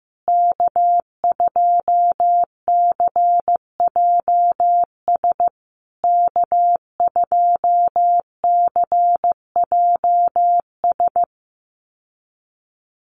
Morse code, heard as a keyed tone, about 15 wpm, 700 Hz. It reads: K2CJS K2CJS
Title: USA Call Signs Repeated 3x 15wpm